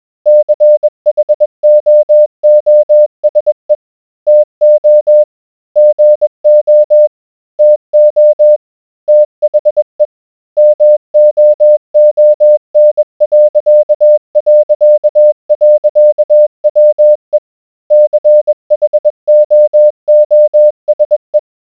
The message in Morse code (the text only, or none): CHOOSE TO GO TO THE MOON...WE CHOOSE